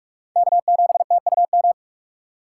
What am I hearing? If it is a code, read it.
Morse code, heard as a keyed tone, about 42 wpm, 700 Hz. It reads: K7TUM
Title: USA Call Signs - 5 Characters - 6x Character Spacing - 45wpm